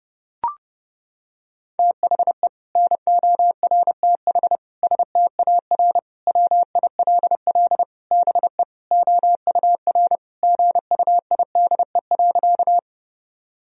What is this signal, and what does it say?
Morse code, heard as a keyed tone, about 32 wpm, 700 Hz. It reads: THE NORTH STAR WILL BE OUR GUIDE.
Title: Sentences from Top 1000 Words - Review Part 3 - 30wpm